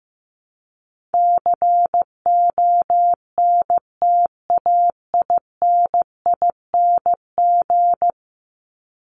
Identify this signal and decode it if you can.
Morse code, heard as a keyed tone, about 15 wpm, 700 Hz. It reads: CONTAINING
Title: ING Suffix - Top 100 Words with Base Words 15wpm